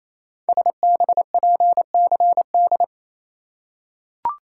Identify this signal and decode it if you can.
Morse code, heard as a keyed tone, about 28 wpm, 700 Hz. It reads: SBPCD
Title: Code Groups - 5 Letters - 28wpm